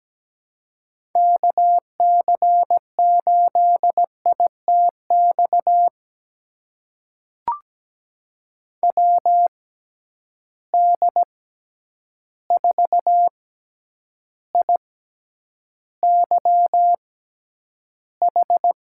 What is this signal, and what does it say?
Morse code, heard as a keyed tone, about 17 wpm, spaced as for 9 wpm, 700 Hz. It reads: KC8ITX WD4IYH
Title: USA Call Signs - 6 Characters - 6x Character Spacing - 17wpm